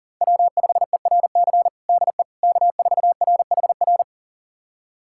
Morse code, heard as a keyed tone, about 40 wpm, 700 Hz. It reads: W5ERC DE K4RHR
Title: USA Callsign DE Callsign 40wpm